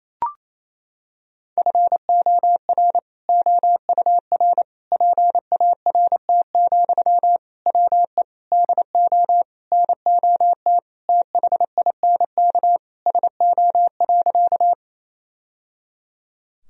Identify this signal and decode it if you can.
Morse code, heard as a keyed tone, about 28 wpm, 700 Hz. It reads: FOR OUR PART, WE DO NOT THINK SO.
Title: Sentences from Top 1000 Words - Review Part 1 - 28wpm